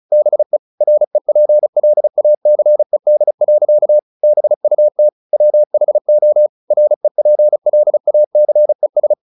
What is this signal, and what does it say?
Morse code, heard as a keyed tone, about 34 wpm, 600 Hz. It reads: BE REPLACED. BUT WHO REPLACES
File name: the old man and the sea / the old man and the sea 1 35wpm